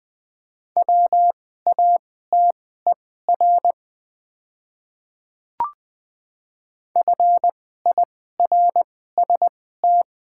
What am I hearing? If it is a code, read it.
Morse code, heard as a keyed tone, about 20 wpm, spaced as for 14 wpm, 700 Hz. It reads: WATER FIRST
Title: 2x Character Spacing - Top 300 Words - 20wpm